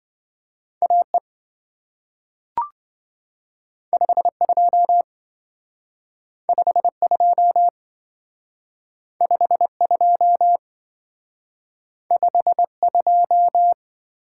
Morse code, heard as a keyed tone, about 27 wpm, 700 Hz. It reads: AE 52 52 52 52